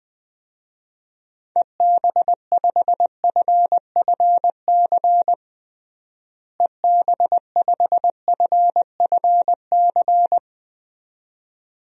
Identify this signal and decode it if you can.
Morse code, heard as a keyed tone, about 20 wpm, 700 Hz. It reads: EB5FFC EB5FFC
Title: DXCC-100 Call Signs - 6 Characters - Repeated 2x - 20wpm